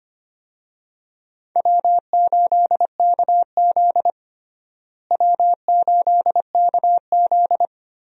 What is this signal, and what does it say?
Morse code, heard as a keyed tone, about 25 wpm, 700 Hz. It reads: W8KZ W8KZ
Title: USA Call Signs - 4 Characters - Repeated 2x - 25wpm